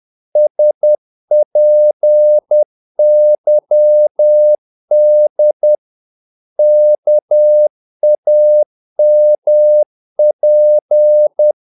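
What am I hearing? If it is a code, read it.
Morse code, heard as a keyed tone, about 10 wpm, 600 Hz. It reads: SPYD KAMP
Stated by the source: Morse practice of short Danish words at 10 wpm, from CW med Gnister.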